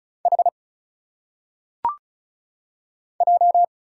Morse code, heard as a keyed tone, about 34 wpm, 700 Hz. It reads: H J
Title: Single Letters - Phonetic Pronunciation 35wpm